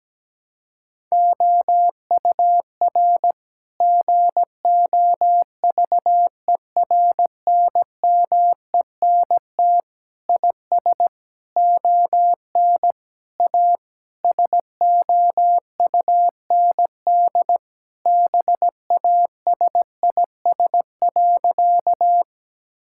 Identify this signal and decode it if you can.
Morse code, heard as a keyed tone, about 17 wpm, 700 Hz. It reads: OUR GOVERNMENT IS ON A SOUND BASIS.
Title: Sentences from Top 1000 Words - Review Part 5 - 17wpm